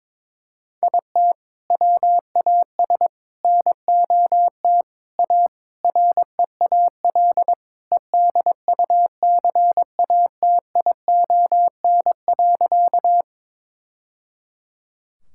Morse code, heard as a keyed tone, about 22 wpm, 700 Hz. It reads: IT WAS NOT A REAL EDUCATION.
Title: Sentences from Top 1000 Words - Review Part 6 - 22wpm